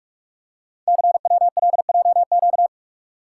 Morse code, heard as a keyed tone, about 44 wpm, 690 Hz. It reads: CWLJQ
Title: Code Groups - 5 Letters - 45wpm